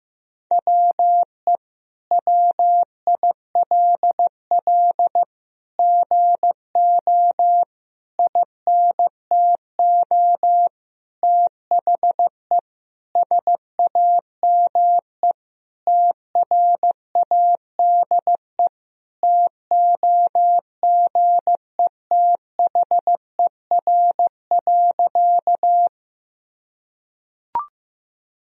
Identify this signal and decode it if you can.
Morse code, heard as a keyed tone, about 15 wpm, 700 Hz. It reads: WE WILL GO INTO THE SAME TRADE TOGETHER.
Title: Sentences from Top 1000 Words - Review Part 4 - 15wpm